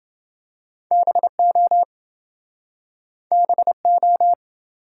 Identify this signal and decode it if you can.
Morse code, heard as a keyed tone, about 28 wpm, 700 Hz. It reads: BO BO